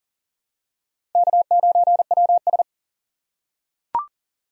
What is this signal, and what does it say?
Morse code, heard as a keyed tone, about 40 wpm, 700 Hz. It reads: K9WS
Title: USA Call Signs - 3x Character Spacing - 40wpm